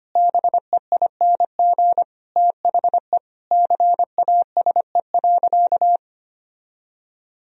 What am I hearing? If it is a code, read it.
Morse code, heard as a keyed tone, about 25 wpm, 700 Hz. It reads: BEING THE CASE.
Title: Sentences from Top 1000 Words - Review Part 3 - 25wpm